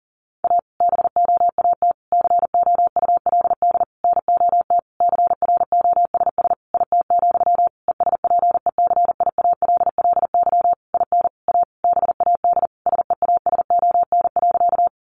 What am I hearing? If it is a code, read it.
Morse code, heard as a keyed tone, about 40 wpm, 700 Hz. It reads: A BOAT COULD NOT CROSS IT, ESPECIALLY IN A BAD SEASON.
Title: Sentences from Top 1400 Words 40wpm